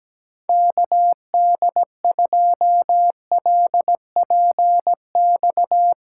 Morse code, heard as a keyed tone, about 17 wpm, 700 Hz. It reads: KD2LPX
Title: USA Callsign DE Callsign - 6 Characters - 17wpm